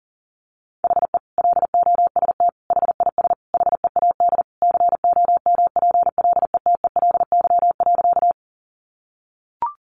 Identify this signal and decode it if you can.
Morse code, heard as a keyed tone, about 40 wpm, 700 Hz. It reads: HE LOST HIS HEAD COMPLETELY.
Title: Sentences from Top 1400 Words 40wpm